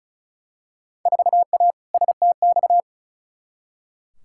Morse code, heard as a keyed tone, about 36 wpm, 700 Hz. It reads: 4A STX
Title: Field Day Exchanges Encore - 35wpm